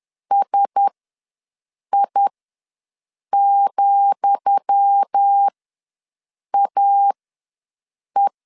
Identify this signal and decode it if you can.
Morse code, heard as a keyed tone, about 10 wpm, spaced as for 6 wpm, 800 Hz. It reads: SI,AE
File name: LEZIONI / 10 SIGNS / 10D COMA ws 10wpm 4314